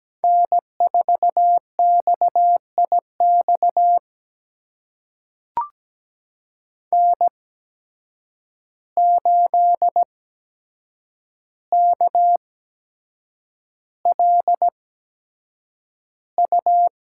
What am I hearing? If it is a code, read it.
Morse code, heard as a keyed tone, about 17 wpm, spaced as for 7 wpm, 700 Hz. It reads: N4XIX N8KLU
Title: USA Call Signs - 5 Characters - 8x Character Spacing - 17wpm